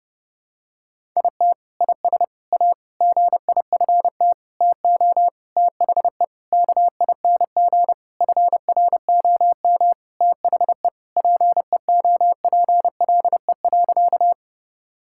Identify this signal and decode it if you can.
Morse code, heard as a keyed tone, about 30 wpm, 700 Hz. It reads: IT IS A GIFT TO THE KING FROM THE PEOPLE.
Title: Sentences from Top 1000 Words - Review Part 6 - 30wpm